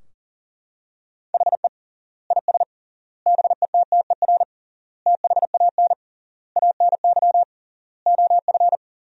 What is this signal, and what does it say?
Morse code, heard as a keyed tone, about 40 wpm, 700 Hz. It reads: HE IS BETTER THAN ANY OF
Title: 3x Word-Spacing - Sentences from Top 400 Words - 40wpm